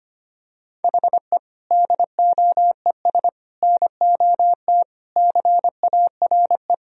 Morse code, heard as a keyed tone, about 25 wpm, 700 Hz. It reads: HE DOES NOT CARE
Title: Sentences from Top 1000 Words - Review Part 1 - 25wpm